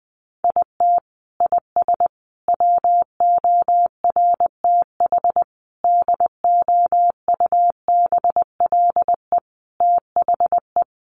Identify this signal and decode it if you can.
Morse code, heard as a keyed tone, about 19 wpm, 700 Hz. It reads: IT IS WORTH DOUBLE THE
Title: Sentences from Top 1400 Words 20wpm